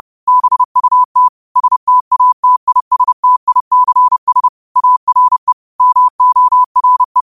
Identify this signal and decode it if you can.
Morse code, heard as a keyed tone, about 30 wpm, 1000 Hz. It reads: BUT STATISTICS ARE MORE